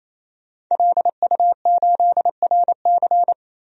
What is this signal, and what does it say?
Morse code, heard as a keyed tone, about 28 wpm, 700 Hz. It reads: LU8RC
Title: DXCC-100 Call Signs - 5 Characters - 28wpm